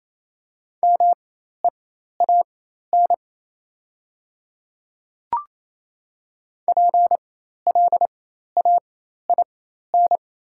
Morse code, heard as a keyed tone, about 28 wpm, spaced as for 13 wpm, 700 Hz. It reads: MEAN PLAIN
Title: 4x Character Spacing - Top 500 Words - 28wpm